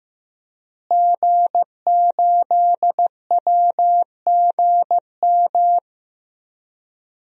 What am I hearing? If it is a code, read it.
Morse code, heard as a keyed tone, about 15 wpm, 700 Hz. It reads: G8WGM
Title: DXCC-100 Call Signs - 5 Characters - Repeated 3x - 15wpm